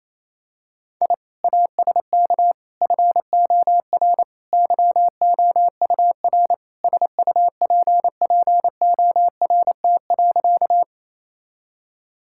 Morse code, heard as a keyed tone, about 28 wpm, 700 Hz. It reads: I ASK FOR YOUR SUPPORT.
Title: Sentences from Top 1000 Words - Review Part 5 - 28wpm